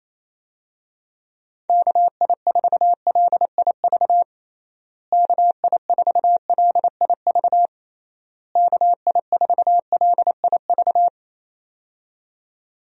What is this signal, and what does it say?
Morse code, heard as a keyed tone, about 28 wpm, 700 Hz. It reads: KI4LIV KI4LIV KI4LIV
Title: USA Call Signs - 6 Characters - Repeated 3x - 28wpm